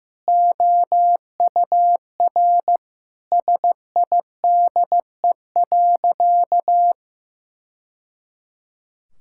Morse code, heard as a keyed tone, about 16 wpm, 700 Hz. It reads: OUR SIDE.
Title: Sentences from Top 1000 Words - Review Part 6 - 15wpm